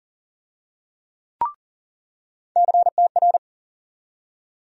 Morse code, heard as a keyed tone, about 40 wpm, 700 Hz. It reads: CTR
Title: USA Call Sign Postfixes - 3 Letters 40wpm